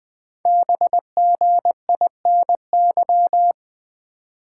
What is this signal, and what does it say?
Morse code, heard as a keyed tone, about 20 wpm, 700 Hz. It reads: BGINY
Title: Code Groups - 5 Letters - 20wpm